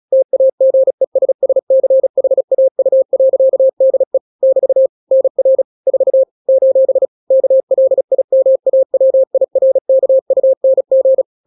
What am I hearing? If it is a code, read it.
Morse code, heard as a keyed tone, about 36 wpm, 530 Hz. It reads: TAGESSCHAU.DE <BT> NR 4 : KLIMAWIRKUNG
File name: daily / 35wpm